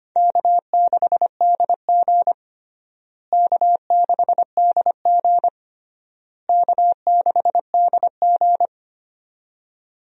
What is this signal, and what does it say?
Morse code, heard as a keyed tone, about 25 wpm, 700 Hz. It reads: K6DG K6DG K6DG
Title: USA Call Signs - 4 Characters - Repeated 3x - 25wpm